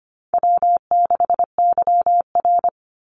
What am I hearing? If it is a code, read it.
Morse code, heard as a keyed tone, about 25 wpm, 700 Hz. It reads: W6YR
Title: Easy Call Signs - 25wpm